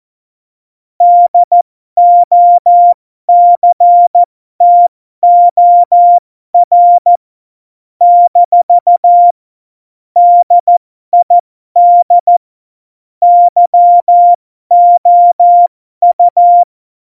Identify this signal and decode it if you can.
Morse code, heard as a keyed tone, about 14 wpm, spaced as for 12 wpm, 700 Hz. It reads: DOCTOR - DID YOU
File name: Morsekurs / Witze